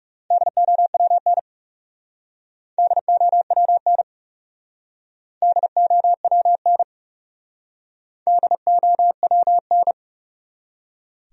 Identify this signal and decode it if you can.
Morse code, heard as a keyed tone, about 38 wpm, 700 Hz. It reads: DOWN DOWN DOWN DOWN